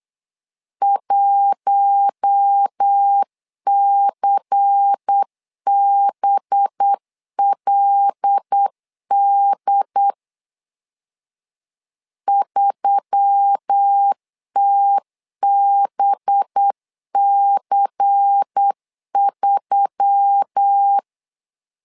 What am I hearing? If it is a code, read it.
Morse code, heard as a keyed tone, about 8 wpm, 800 Hz. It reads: 1CBLD 3TBC3
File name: LEZIONI / 5 LBVC / 5 LBVC 8wpm 2935